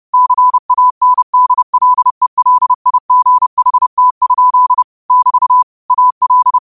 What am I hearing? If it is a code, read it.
Morse code, heard as a keyed tone, about 30 wpm, 1000 Hz. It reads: CANDLELIGHT? <BT> AL
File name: qotdfiles / 30wpm